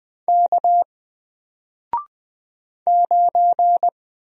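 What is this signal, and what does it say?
Morse code, heard as a keyed tone, about 20 wpm, 700 Hz. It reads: K 9
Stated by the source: Single letter and number Morse drill at 20 wpm, from N4LCD.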